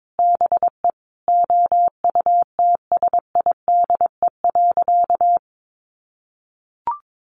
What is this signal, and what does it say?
Morse code, heard as a keyed tone, about 22 wpm, 700 Hz. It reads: BE OUTSIDE.
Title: Sentences from Top 1200 Words 22wpm